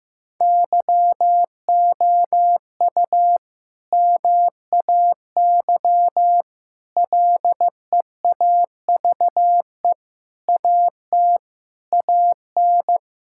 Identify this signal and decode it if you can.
Morse code, heard as a keyed tone, about 15 wpm, 700 Hz. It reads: YOU MAY LEAVE AT AN
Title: Sentences from Top 1000 Words - Review Part 5 - 15wpm